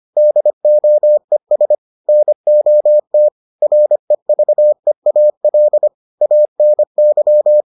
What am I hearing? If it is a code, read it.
Morse code, heard as a keyed tone, about 25 wpm, 600 Hz. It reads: DOES NOT REVEAL ANY